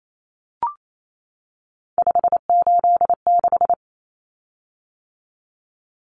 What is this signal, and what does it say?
Morse code, heard as a keyed tone, about 28 wpm, 700 Hz. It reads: 586
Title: Common Cut Numbers - 3-Digit 28wpm